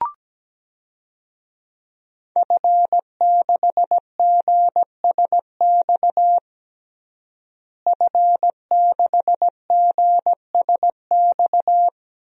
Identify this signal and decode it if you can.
Morse code, heard as a keyed tone, about 17 wpm, 700 Hz. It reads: F6GSX F6GSX
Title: DXCC-100 Call Signs - 5 Characters - Repeated 2x - 17wpm